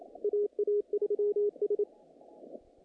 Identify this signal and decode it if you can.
Morse code, heard as a keyed tone, about 28 wpm, 410 Hz, with radio noise. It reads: AA3S
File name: contests / naqp-cw-k5tr-op-k5pi- -2100